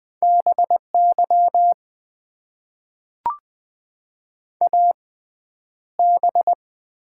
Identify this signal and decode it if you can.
Morse code, heard as a keyed tone, about 20 wpm, spaced as for 10 wpm, 700 Hz. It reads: BY AB
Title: Top 90 QSO Elements - 6x Character Spacing - 20wpm